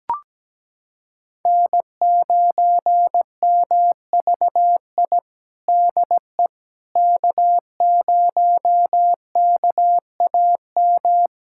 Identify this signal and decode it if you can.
Morse code, heard as a keyed tone, about 17 wpm, 700 Hz. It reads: N9MVI DE K0KAM
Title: USA Callsign DE Callsign - 5 Characters - 17wpm